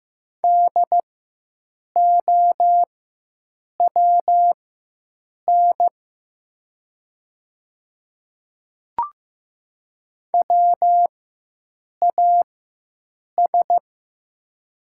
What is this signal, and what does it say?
Morse code, heard as a keyed tone, about 15 wpm, spaced as for 7 wpm, 700 Hz. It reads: DOWN WAS
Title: 4x Character Spacing - Top 500 Words - 15wpm